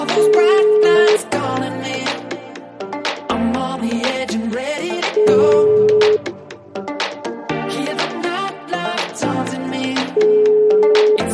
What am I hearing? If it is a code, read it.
Ringback tone (Europe, 425 Hz), with noise.